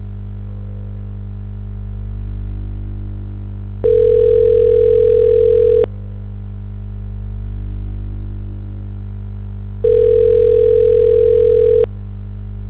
Ringback tone (North America), 440 + 480 Hz.